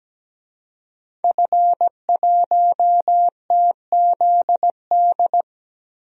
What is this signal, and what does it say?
Morse code, heard as a keyed tone, about 17 wpm, 700 Hz. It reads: F1TZD
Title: DXCC-100 Call Signs - 5 Characters - 17wpm